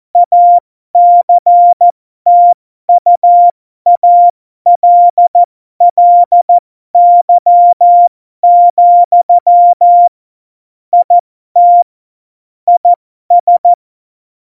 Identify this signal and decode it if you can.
Morse code, heard as a keyed tone, about 14 wpm, spaced as for 12 wpm, 700 Hz. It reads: ACTUALLY, IT IS
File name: Morsekurs / Witze